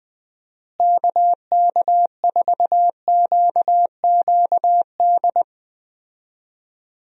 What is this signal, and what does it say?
Morse code, heard as a keyed tone, about 20 wpm, 700 Hz. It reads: KK4QQD
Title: USA Call Signs - 6 Characters - Repeated 2x - 20wpm